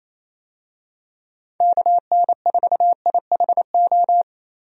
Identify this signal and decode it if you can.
Morse code, heard as a keyed tone, about 28 wpm, 700 Hz. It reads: KN4IHO
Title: USA Call Signs - 6 Characters - 28wpm